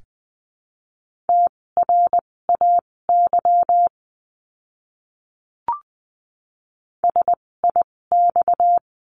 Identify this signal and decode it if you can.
Morse code, heard as a keyed tone, about 20 wpm, spaced as for 16 wpm, 700 Hz. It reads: TRAY SIX
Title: 256 Lesson 42a - Words 12-20wpm